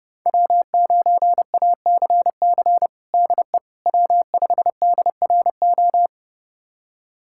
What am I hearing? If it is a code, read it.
Morse code, heard as a keyed tone, about 30 wpm, 700 Hz. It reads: W9ACC DE W5DRO
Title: USA Callsign DE Callsign - 5 Characters - 30wpm